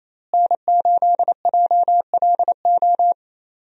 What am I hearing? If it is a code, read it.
Morse code, heard as a keyed tone, about 28 wpm, 700 Hz. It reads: N8JLO
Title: USA Call Signs - 5 Characters - Repeated 3x - 28wpm